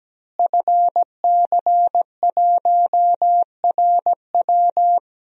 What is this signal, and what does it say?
Morse code, heard as a keyed tone, about 17 wpm, 700 Hz. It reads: FC1RW
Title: DXCC-100 Call Signs - 5 Characters - Repeated 3x - 17wpm